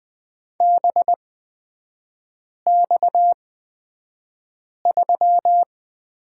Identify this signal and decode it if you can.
Morse code, heard as a keyed tone, about 20 wpm, spaced as for 5 wpm, 700 Hz. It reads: BX3
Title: 229 Lesson 37e - Letter x - Three Characters 20wpm